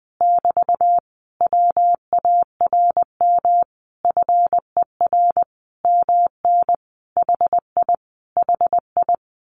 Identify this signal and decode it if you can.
Morse code, heard as a keyed tone, about 20 wpm, 700 Hz. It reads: <BT> WARM FER MN HI HI
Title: Common QSO Phrases Encore 20wpm